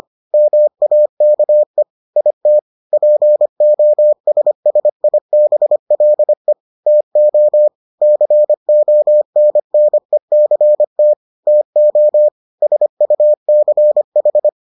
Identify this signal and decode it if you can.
Morse code, heard as a keyed tone, about 25 wpm, 600 Hz. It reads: MAKE IT POSSIBLE TO CONNECT TO SUCH